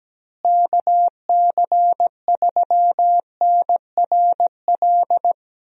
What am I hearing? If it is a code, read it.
Morse code, heard as a keyed tone, about 17 wpm, 700 Hz. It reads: KC3NRL
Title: USA Call Signs - 6 Characters - Repeated 3x - 17wpm